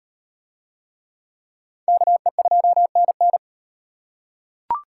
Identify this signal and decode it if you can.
Morse code, heard as a keyed tone, about 38 wpm, 700 Hz. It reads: KE2NN